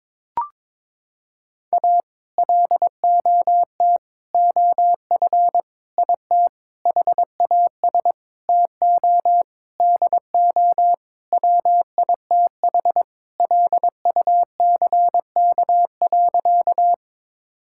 Morse code, heard as a keyed tone, about 22 wpm, 700 Hz. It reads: A LOT OF IT HAS TO DO WITH LUCK.
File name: practice / 22wpm / Sentences with 8-Word Phrases - 22wpm